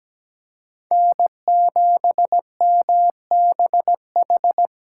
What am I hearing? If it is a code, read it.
Morse code, heard as a keyed tone, about 17 wpm, 700 Hz. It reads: N7MBH
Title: USA Call Signs - 5 Characters - 4x Character Spacing - 17wpm